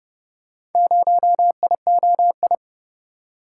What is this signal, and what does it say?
Morse code, heard as a keyed tone, about 29 wpm, 700 Hz. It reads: 0IOI